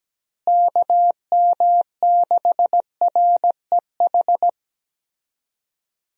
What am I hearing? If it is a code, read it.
Morse code, heard as a keyed tone, about 17 wpm, 700 Hz. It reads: KM6REH
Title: USA Call Signs - 6 Characters - Repeated 2x - 17wpm